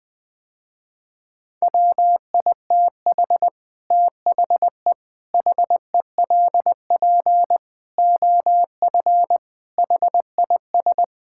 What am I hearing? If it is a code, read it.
Morse code, heard as a keyed tone, about 19 wpm, 700 Hz. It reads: WITH THE HELP OF HIS
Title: Sentences with 4-Word Phrases - Spoken Beginning - 20wpm